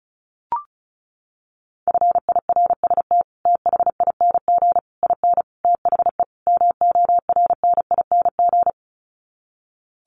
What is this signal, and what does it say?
Morse code, heard as a keyed tone, about 34 wpm, 700 Hz. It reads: FIRST THING IN THE MORNING
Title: Sets of 5 Words Encore2 35wpm